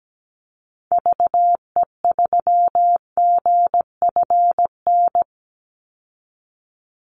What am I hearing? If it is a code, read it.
Morse code, heard as a keyed tone, about 17 wpm, 700 Hz. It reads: VE3GFN
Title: Call Signs Repeated 2x - 17wpm